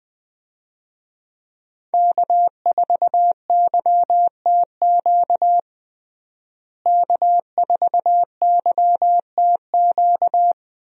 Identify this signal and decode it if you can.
Morse code, heard as a keyed tone, about 20 wpm, 700 Hz. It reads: K4YTQ K4YTQ
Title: USA Call Signs - 5 Characters - Repeated 3x - 20wpm